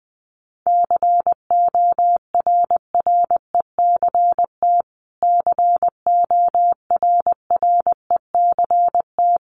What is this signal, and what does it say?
Morse code, heard as a keyed tone, about 20 wpm, 700 Hz. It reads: CORRECT CORRECT
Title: Top 500 Words Repeated 20wpm